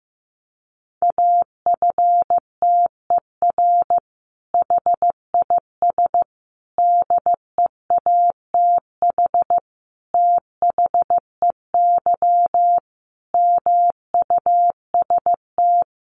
Morse code, heard as a keyed tone, about 15 wpm, 700 Hz. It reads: AFTER HIS DEATH THEY MUST
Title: Sentences from Top 1100 Words 15wpm